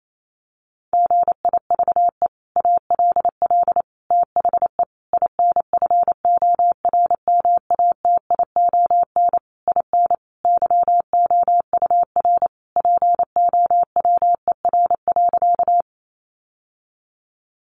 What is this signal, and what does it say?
Morse code, heard as a keyed tone, about 28 wpm, 700 Hz. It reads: GIVE ALL THE INFORMATION IN YOUR POWER.
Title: Sentences from Top 600 Words 28wpm